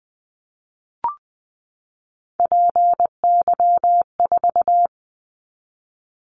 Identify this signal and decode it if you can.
Morse code, heard as a keyed tone, about 20 wpm, 700 Hz. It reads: PY4
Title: Callsign Prefixes for Top 100 DXCC - 20wpm